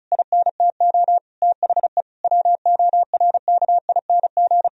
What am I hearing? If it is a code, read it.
Morse code, heard as a keyed tone, about 36 wpm, 700 Hz. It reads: INTO THE WORKING
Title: Sentences from Top 1000 Words - Review Part 4 - 35wpm